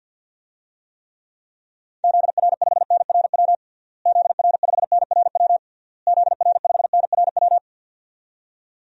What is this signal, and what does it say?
Morse code, heard as a keyed tone, about 46 wpm, 700 Hz. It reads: ZR5NRW ZR5NRW ZR5NRW
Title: DXCC-100 Call Signs - 6 Characters - Repeated 3x - 50wpm